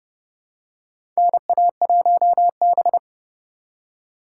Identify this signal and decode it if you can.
Morse code, heard as a keyed tone, about 32 wpm, 700 Hz. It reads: NA1B
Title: USA Call Signs - 4 Characters - 3x Character Spacing - 30wpm